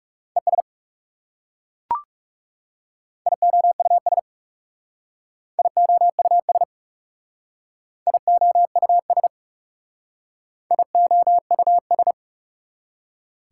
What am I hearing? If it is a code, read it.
Morse code, heard as a keyed tone, about 37 wpm, 690 Hz. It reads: ES IOUS IOUS IOUS IOUS
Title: Speed-Racing Prefixes and Suffixes 30wpm